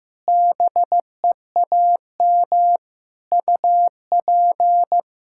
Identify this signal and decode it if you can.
Morse code, heard as a keyed tone, about 16 wpm, 700 Hz. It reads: BEAM UP
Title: Practice QSO - Exchange 2 - Info - 15wpm